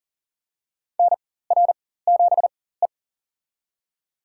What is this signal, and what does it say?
Morse code, heard as a keyed tone, about 38 wpm, spaced as for 18 wpm, 700 Hz. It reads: NR7E
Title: USA Call Signs - 4x Character Spacing - 40wpm